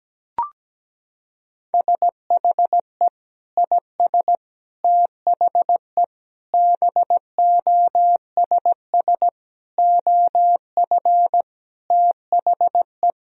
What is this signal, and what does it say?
Morse code, heard as a keyed tone, about 17 wpm, 700 Hz. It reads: SHE IS THE BOSS OF THE
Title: Sentences from Top 1000 Words - Review Part 4 - 17wpm